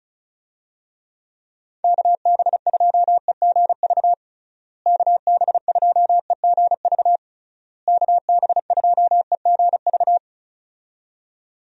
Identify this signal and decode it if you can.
Morse code, heard as a keyed tone, about 36 wpm, 700 Hz. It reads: KB2EGV KB2EGV KB2EGV
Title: USA Call Signs - 6 Characters - Repeated 3x - 35wpm